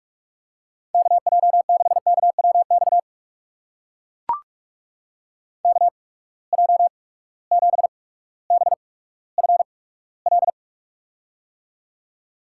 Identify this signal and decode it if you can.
Morse code, heard as a keyed tone, about 44 wpm, spaced as for 19 wpm, 700 Hz. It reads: KJ6KWX KJ7BFL
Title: USA Call Signs - 6 Characters - 8x Character Spacing - 45wpm